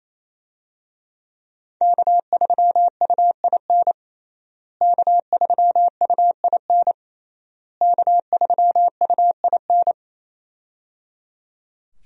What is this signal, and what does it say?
Morse code, heard as a keyed tone, about 28 wpm, 700 Hz. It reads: K3UIN K3UIN K3UIN
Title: USA Call Signs - 5 Characters - Repeated 3x - 28wpm